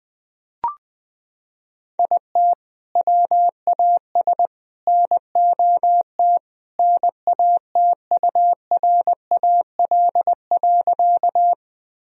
Morse code, heard as a keyed tone, about 21 wpm, 700 Hz. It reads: IT WAS NOT NATURAL.
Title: Sentences from Top 1000 Words - Review Part 4 - 20wpm